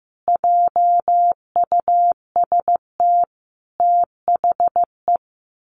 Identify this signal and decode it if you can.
Morse code, heard as a keyed tone, about 15 wpm, 700 Hz. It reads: JUST THE
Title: Sentences from Top 1500 Words 15wpm